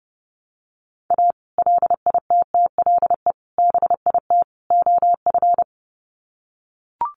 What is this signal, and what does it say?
Morse code, heard as a keyed tone, about 30 wpm, 700 Hz. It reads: A LITTLE BIT OF
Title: Sets of 4 Words 30wpm